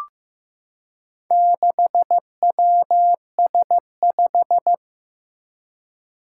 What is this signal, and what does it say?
Morse code, heard as a keyed tone, about 15 wpm, 700 Hz. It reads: 6WS5
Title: Four Challenging Letter-Number Combinations 15wpm